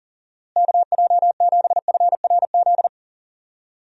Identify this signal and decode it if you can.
Morse code, heard as a keyed tone, about 40 wpm, 700 Hz. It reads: KJ7FRZ
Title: USA Call Signs - 6 Characters - Repeated 3x - 40wpm